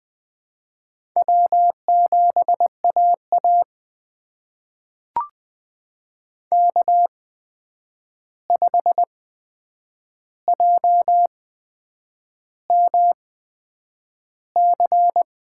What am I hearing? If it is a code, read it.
Morse code, heard as a keyed tone, about 20 wpm, spaced as for 8 wpm, 700 Hz. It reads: W7AA K5JMC
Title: USA Call Signs - 8x Character Spacing - 20wpm